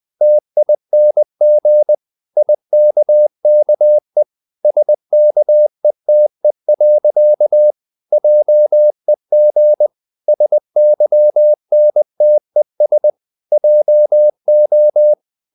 Morse code, heard as a keyed tone, about 20 wpm, 600 Hz. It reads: TING IKKE SKETE. JEG SYNTES JO
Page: Ferie 20 wpm